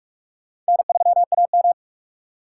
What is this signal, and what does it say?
Morse code, heard as a keyed tone, about 55 wpm, 700 Hz. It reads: N3AM
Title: Speed Racing - USA 1x2 Call Signs 45wpm